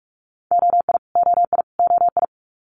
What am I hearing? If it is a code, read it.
Morse code, heard as a keyed tone, about 46 wpm, 690 Hz. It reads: OI OI OI
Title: Multi-letter Phonemes Repeated 3x 45wpm